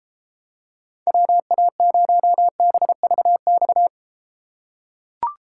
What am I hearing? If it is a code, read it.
Morse code, heard as a keyed tone, about 32 wpm, 700 Hz. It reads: WA0BVX